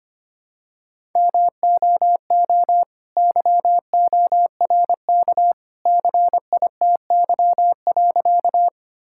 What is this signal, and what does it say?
Morse code, heard as a keyed tone, about 25 wpm, 700 Hz. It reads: MOO YORK CITY.
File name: practice / 25wpm / All-Age Appropriate Jokes - 25wpm